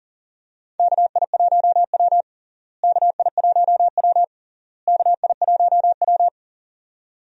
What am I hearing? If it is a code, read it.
Morse code, heard as a keyed tone, about 40 wpm, 700 Hz. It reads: KI1W KI1W KI1W
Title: USA Call Signs - 4 Characters - Repeated 3x - 40wpm